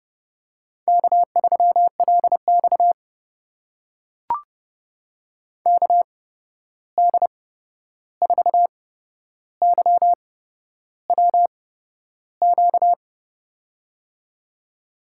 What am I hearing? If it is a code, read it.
Morse code, heard as a keyed tone, about 30 wpm, spaced as for 11 wpm, 700 Hz. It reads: K3LX KD4YWQ
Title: USA Call Signs - 8x Character Spacing - 30wpm